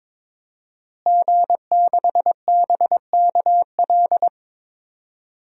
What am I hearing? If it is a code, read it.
Morse code, heard as a keyed tone, about 21 wpm, 700 Hz. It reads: G6BKL
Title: DXCC-100 Call Signs - 5 Characters - Repeated 3x - 22wpm